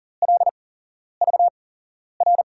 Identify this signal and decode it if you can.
Morse code, heard as a keyed tone, about 41 wpm, spaced as for 11 wpm, 700 Hz. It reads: LVR
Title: USA Call Signs - 6 Characters - 8x Character Spacing - 40wpm